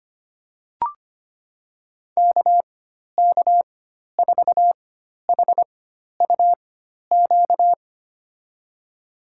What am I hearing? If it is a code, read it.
Morse code, heard as a keyed tone, about 25 wpm, spaced as for 12 wpm, 700 Hz. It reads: KK4HUQ
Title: USA Call Signs - 6 Characters - 4x Character Spacing - 25wpm